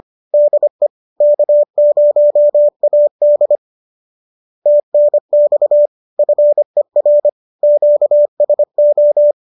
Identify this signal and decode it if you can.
Morse code, heard as a keyed tone, about 25 wpm, 590 Hz. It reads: DE K0AD TNX FER QSO